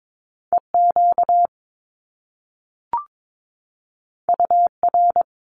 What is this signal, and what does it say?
Morse code, heard as a keyed tone, about 22 wpm, 700 Hz. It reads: EQ UR
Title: Top Two Letter Combinations in Words 22wpm